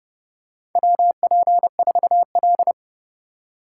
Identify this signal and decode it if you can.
Morse code, heard as a keyed tone, about 30 wpm, 700 Hz. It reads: WP4L
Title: USA Call Signs - 4x Character Spacing - 30wpm